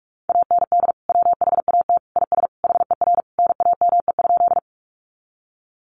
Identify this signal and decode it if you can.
Morse code, heard as a keyed tone, about 44 wpm, 700 Hz. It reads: AND WHAT IS HER NAME?
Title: Sentences from Top 200 Words 45wpm